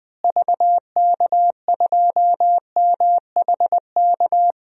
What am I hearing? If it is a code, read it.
Morse code, heard as a keyed tone, about 20 wpm, 700 Hz. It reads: VK2MHK
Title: DXCC-100 Call Signs - 6 Characters - Repeated 2x - 20wpm